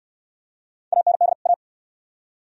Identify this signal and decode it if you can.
Morse code, heard as a keyed tone, about 69 wpm, 690 Hz. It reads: AND A